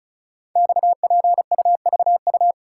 Morse code, heard as a keyed tone, about 36 wpm, 700 Hz. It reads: XPUVU
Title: Code Groups - 5 Letters - 35wpm